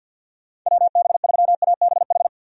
Morse code, heard as a keyed tone, about 45 wpm, 700 Hz. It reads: WB3ABH
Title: USA Call Signs - 6 Characters - 50wpm